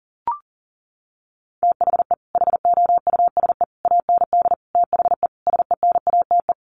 Morse code, heard as a keyed tone, about 40 wpm, 700 Hz. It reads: THE HOUSE AND THE SENATE
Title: Sets of 5 Words Encore2 40wpm